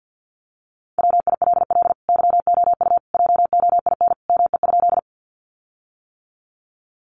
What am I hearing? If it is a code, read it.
Morse code, heard as a keyed tone, about 50 wpm, 700 Hz. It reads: WILL YOU JOIN ME?
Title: Sentences from Top 1400 Words 50wpm